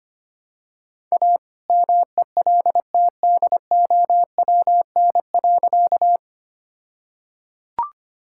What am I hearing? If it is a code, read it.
Morse code, heard as a keyed tone, about 25 wpm, 700 Hz. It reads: A MELTDOWN.
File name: practice / 25wpm / All-Age Appropriate Jokes - 25wpm